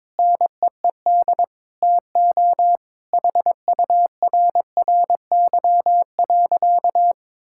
Morse code, heard as a keyed tone, about 22 wpm, 700 Hz. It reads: NEED TO HURRY.
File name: practice / 22wpm / Sentences with 4-Word Phrases - 22wpm